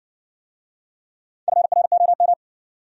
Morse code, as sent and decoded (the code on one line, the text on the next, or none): ..-. .-. --- --
FROM